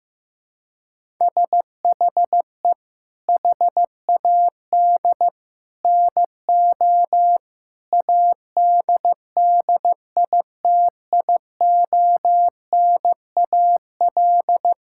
Morse code, heard as a keyed tone, about 15 wpm, 700 Hz. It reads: SHE HAD NO ADDITIONAL
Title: Sentences from Top 1000 Words - Review Part 6 - 15wpm